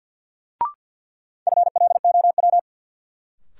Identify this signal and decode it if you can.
Morse code, heard as a keyed tone, about 52 wpm, 700 Hz. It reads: FLOW
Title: Sentences from Top 1200 Words - New Word Drill - 50wpm